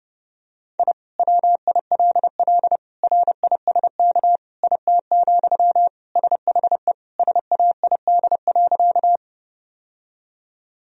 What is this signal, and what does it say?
Morse code, heard as a keyed tone, about 32 wpm, 700 Hz. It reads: I WILL RISK IT, SHE SAID.
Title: Sentences from Top 1000 Words - Review Part 4 - 30wpm